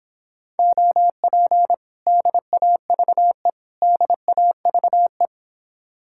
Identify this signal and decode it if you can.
Morse code, heard as a keyed tone, about 31 wpm, 700 Hz. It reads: OP DAVE DAVE